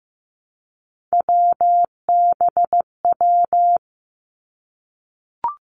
Morse code, heard as a keyed tone, about 15 wpm, 700 Hz. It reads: WBW
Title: Three Challenging Letter-Number Combinations 15wpm